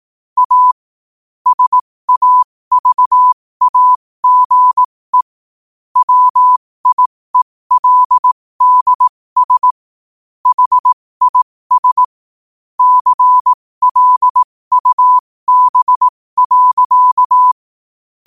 Morse code, heard as a keyed tone, about 18 wpm, spaced as for 15 wpm, 1000 Hz. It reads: A SAVAGE WIELDS HIS CLUB.
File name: qotdfiles / 15wpm